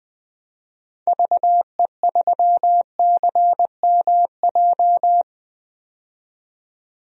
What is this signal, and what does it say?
Morse code, heard as a keyed tone, about 20 wpm, 700 Hz. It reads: VE3CMJ
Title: DXCC-100 Call Signs - 6 Characters - 20wpm